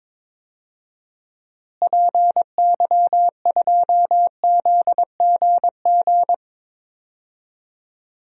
Morse code, heard as a keyed tone, about 21 wpm, 700 Hz. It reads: PY2ZGG
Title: DXCC-100 Call Signs - 6 Characters - Repeated 2x - 22wpm